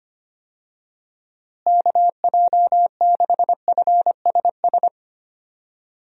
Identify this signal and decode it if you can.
Morse code, heard as a keyed tone, about 25 wpm, 700 Hz. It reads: KJ6FSS
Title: USA Call Signs - 6 Characters - Repeated 2x - 25wpm